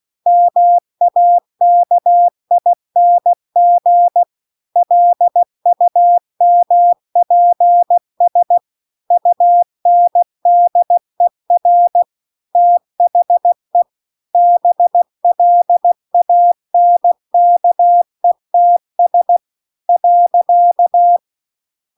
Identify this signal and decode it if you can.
Morse code, heard as a keyed tone, about 16 wpm, 700 Hz. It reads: MAKING LUMPS UNDER THE BLANKETS .